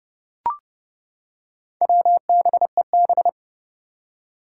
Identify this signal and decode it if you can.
Morse code, heard as a keyed tone, about 30 wpm, 700 Hz. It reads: WBEB
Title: Four Challenging Letter-Number Combinations 30wpm